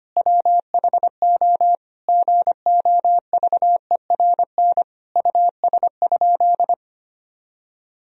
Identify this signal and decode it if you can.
Morse code, heard as a keyed tone, about 25 wpm, 700 Hz. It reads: WHO GOVERN US?
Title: Sentences from Top 1000 Words - Review Part 2 - 25wpm